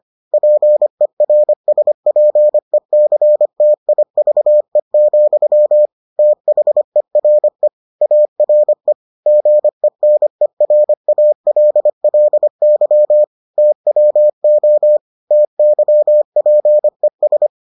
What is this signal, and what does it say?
Morse code, heard as a keyed tone, about 25 wpm, 600 Hz. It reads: PERSPECTIVE, THERE ARE GENERALLY TWO TYPES